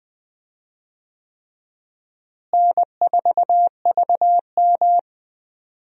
Morse code, heard as a keyed tone, about 20 wpm, 700 Hz. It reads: N4VM
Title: USA Call Signs - 4 Characters - Repeated 3x - 20wpm